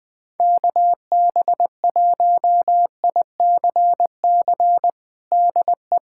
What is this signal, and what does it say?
Morse code, heard as a keyed tone, about 20 wpm, 700 Hz. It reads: KB1ICC DE
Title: USA Callsign DE Callsign 20wpm